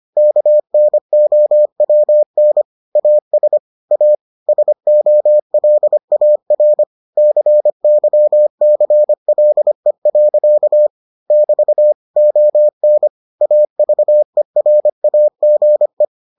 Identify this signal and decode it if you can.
Morse code, heard as a keyed tone, about 25 wpm, 600 Hz. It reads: KNOWN AS A SOLAR CYCLE. <BT> ON AVERAGE